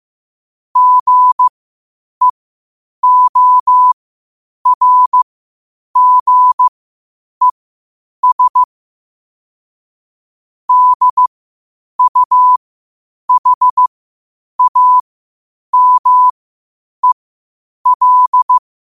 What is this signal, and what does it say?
Morse code, heard as a keyed tone, about 15 wpm, spaced as for 8 wpm, 1000 Hz. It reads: GEORGES DUHAMEL
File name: qotdfiles / 8wpm